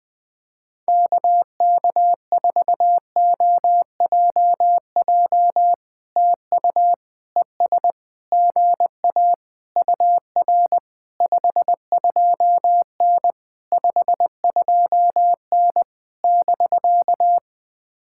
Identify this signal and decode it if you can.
Morse code, heard as a keyed tone, about 20 wpm, 700 Hz. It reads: KK4OJJ TU ES GA UR 52N 52N <BK>
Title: POTA QSO Practice - 20wpm